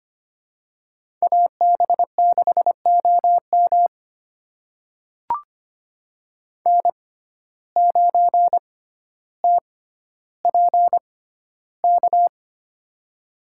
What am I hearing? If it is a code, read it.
Morse code, heard as a keyed tone, about 25 wpm, spaced as for 13 wpm, 700 Hz. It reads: AB6OM N9TPK
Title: USA Call Signs - 5 Characters - 6x Character Spacing - 25wpm